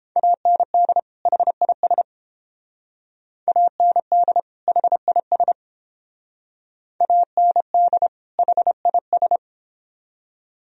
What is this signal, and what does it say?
Morse code, heard as a keyed tone, about 28 wpm, 700 Hz. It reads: AND HIS AND HIS AND HIS